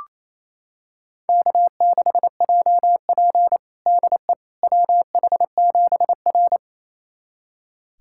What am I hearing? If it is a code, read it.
Morse code, heard as a keyed tone, about 28 wpm, 700 Hz. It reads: K6JP DE WH7R
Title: USA Callsign DE Callsign - 4 Characters - 28wpm